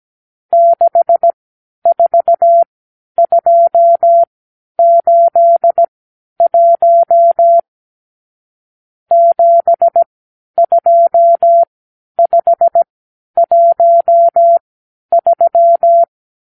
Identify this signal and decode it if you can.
Morse code, heard as a keyed tone, about 17 wpm, spaced as for 10 wpm, 680 Hz. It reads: 64281 72513
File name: arrl / Review (1-0)